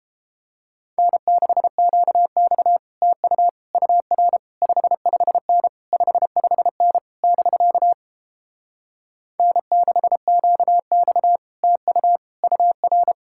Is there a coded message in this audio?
Yes: Morse code from 1.0 to 13.1 s, heard as a keyed tone, about 33 wpm, 700 Hz. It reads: N6QX TU UR 55N 55N <BK> N6QX TU UR